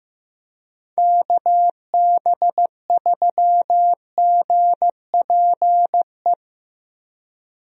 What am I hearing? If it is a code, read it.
Morse code, heard as a keyed tone, about 15 wpm, 700 Hz. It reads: KB3GPE
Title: USA Call Signs - 6 Characters - Repeated 2x - 15wpm